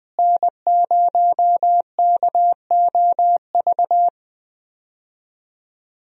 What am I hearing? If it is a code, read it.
Morse code, heard as a keyed tone, about 20 wpm, 700 Hz. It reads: N0KOV
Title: USA Call Signs - 5 Characters - Repeated 3x - 20wpm